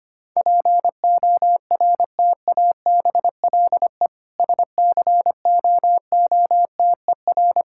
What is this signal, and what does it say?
Morse code, heard as a keyed tone, about 25 wpm, 700 Hz. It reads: PORTABLE SCOOTER
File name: practice / 25wpm / Difficult Words - Speed-Racing with Hints - 25wpm